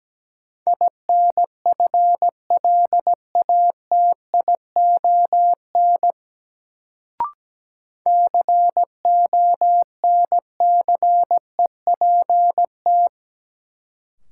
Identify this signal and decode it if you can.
Morse code, heard as a keyed tone, about 17 wpm, 700 Hz. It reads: INFLATION CONCEPT
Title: Sentences from Top 1100 Words - New Word Drill - 17wpm